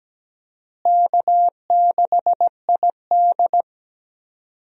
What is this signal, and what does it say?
Morse code, heard as a keyed tone, about 17 wpm, 700 Hz. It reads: K6ID
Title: USA Call Signs - 4 Characters - Repeated 2x - 17wpm